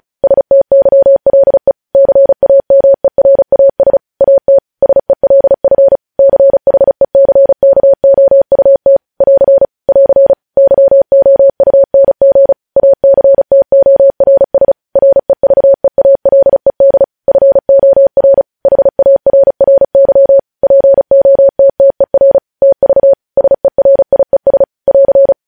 Morse code, heard as a keyed tone, about 36 wpm, 550 Hz. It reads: STYLE CAMERAS AT SELF CHECKOUT <AR> <AR> YOUNG ACTORS REVEALED FOR HARRY POTTER TV SERIES <AR>
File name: podcast / News headlines-1x-35x35